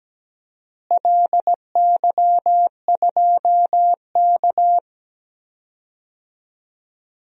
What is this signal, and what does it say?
Morse code, heard as a keyed tone, about 17 wpm, 700 Hz. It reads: LY2K
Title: DXCC-100 Call Signs - 4 Characters - Repeated 3x - 17wpm